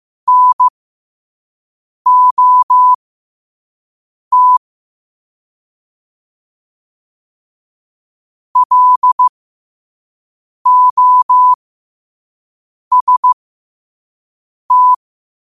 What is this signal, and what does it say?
Morse code, heard as a keyed tone, about 15 wpm, spaced as for 5 wpm, 1000 Hz. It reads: NOT LOST
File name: qotdfiles / 5wpm